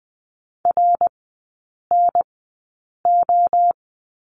Morse code, heard as a keyed tone, about 20 wpm, spaced as for 8 wpm, 700 Hz. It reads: RNO
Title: 053 Lesson 10e - Letter r - Three Characters 20wpm